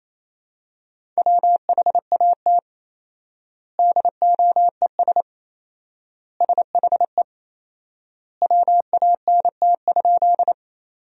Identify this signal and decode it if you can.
Morse code, heard as a keyed tone, about 28 wpm, 700 Hz. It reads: WHAT DOES SHE WANT?
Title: 4x Word-Spacing - Sentences from Top 200 Words - 28wpm